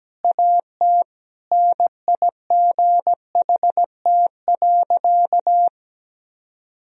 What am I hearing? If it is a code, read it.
Morse code, heard as a keyed tone, about 17 wpm, 700 Hz. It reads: AT NIGHT.
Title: Sentences from Top 1000 Words - Review Part 1 - 17wpm